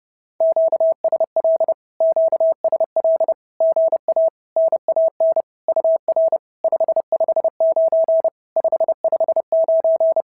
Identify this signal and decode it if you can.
Morse code, heard as a keyed tone, about 30 wpm, 690 Hz. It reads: QSL QSL GA NAN UR 559 559
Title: SOTA QSO Practice 30wpm